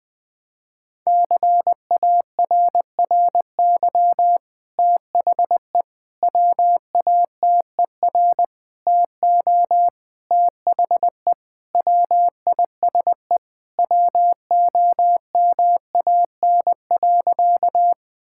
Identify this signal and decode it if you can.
Morse code, heard as a keyed tone, about 20 wpm, 700 Hz. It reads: CARRY THE WATER TO THE WISE WOMAN.
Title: Sentences from Top 1600 Words 20wpm